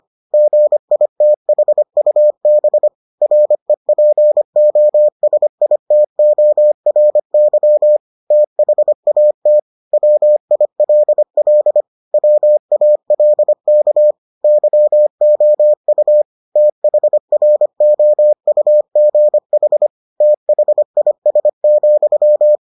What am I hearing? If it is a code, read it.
Morse code, heard as a keyed tone, about 25 wpm, 600 Hz. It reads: GITHUB REPOSITORY THAT WILL WALK YOU THROUGH THIS,